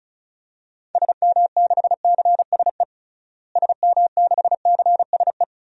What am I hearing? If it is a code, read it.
Morse code, heard as a keyed tone, about 34 wpm, 700 Hz. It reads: SM6CSE SM6CSE
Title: DXCC-100 Call Signs - 6 Characters - Repeated 2x - 35wpm